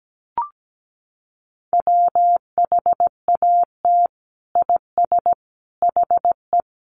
Morse code, heard as a keyed tone, about 17 wpm, 700 Hz. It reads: WHAT IS HE
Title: Sentences from Top 100 Words 17wpm